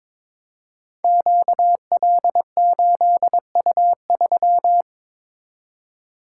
Morse code, heard as a keyed tone, about 22 wpm, 700 Hz. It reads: QL8U3
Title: Code Groups - 5 Letter:Numbers - 22wpm